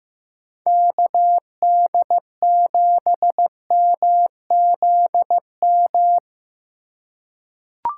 Morse code, heard as a keyed tone, about 15 wpm, 700 Hz. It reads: KD7MZM
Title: USA Call Signs - 6 Characters - 3x Character Spacing - 15wpm